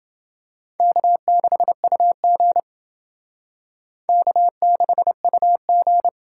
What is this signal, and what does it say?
Morse code, heard as a keyed tone, about 29 wpm, 700 Hz. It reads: K6UG K6UG